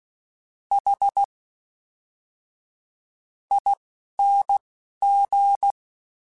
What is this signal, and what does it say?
Morse code, heard as a keyed tone, about 16 wpm, spaced as for 7 wpm, 800 Hz. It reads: HING
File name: Special INGs / Special INGs-mp3s / Fast ing @10